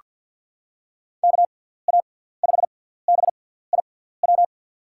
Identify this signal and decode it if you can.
Morse code, heard as a keyed tone, about 47 wpm, 700 Hz. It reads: K A 5 B I W
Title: USA Call Signs - 6 Characters - 6x Character Spacing - 50wpm